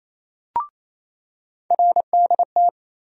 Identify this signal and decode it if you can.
Morse code, heard as a keyed tone, about 28 wpm, 700 Hz. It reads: RDT
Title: USA Call Sign Postfixes - 3 Letters 28wpm